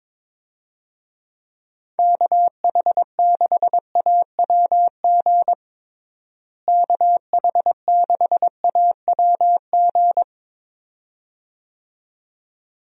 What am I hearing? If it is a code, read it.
Morse code, heard as a keyed tone, about 22 wpm, 700 Hz. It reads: KH6AWG KH6AWG
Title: DXCC-100 Call Signs - 6 Characters - Repeated 2x - 22wpm